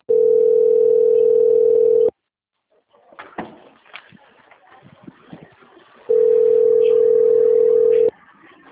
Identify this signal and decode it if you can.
Ringback tone (North America), 440 + 480 Hz.
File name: out- - - - - .21472